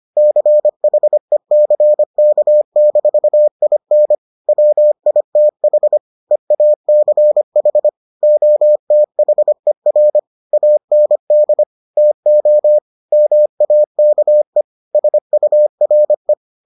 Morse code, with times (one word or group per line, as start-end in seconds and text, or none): 0.2-4.1 CHECK-IN
4.5-6.0 WITH
6.3-7.9 EACH
8.2-10.2 OTHER
10.5-11.6 AND
12.0-12.8 TO
13.1-14.6 MAKE
14.9-16.3 SURE